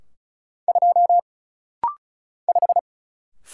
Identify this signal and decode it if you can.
Morse code, heard as a keyed tone, about 34 wpm, 700 Hz. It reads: 2 5
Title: Single Letter-Number Alt - Rapid-Fire 35wpm